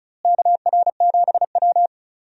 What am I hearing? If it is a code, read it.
Morse code, heard as a keyed tone, about 36 wpm, 700 Hz. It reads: KR7W
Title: USA Call Signs - 4 Characters - Repeated 3x - 35wpm